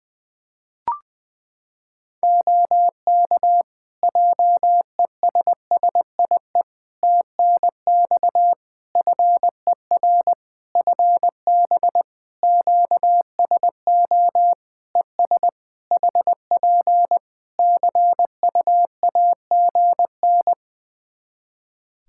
Morse code, heard as a keyed tone, about 20 wpm, 700 Hz. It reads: OK JESSIE TNX FER FB QSO ES HP CUAGN
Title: Practice QSO - Exchange 4 - Ending - 20wpm